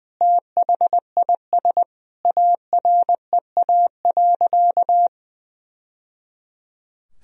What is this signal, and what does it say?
Morse code, heard as a keyed tone, about 20 wpm, 700 Hz. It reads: THIS AREA.
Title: Sentences from Top 1000 Words - Review Part 2 - 20wpm